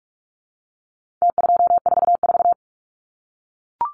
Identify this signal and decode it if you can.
Morse code, heard as a keyed tone, about 44 wpm, 700 Hz. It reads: T244
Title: Numbers - Contest Style - 4-Digit 45wpm